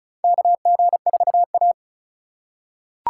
Morse code, heard as a keyed tone, about 36 wpm, 700 Hz. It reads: KG4A
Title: USA Call Signs - 4 Characters - 2x Character Spacing - 35wpm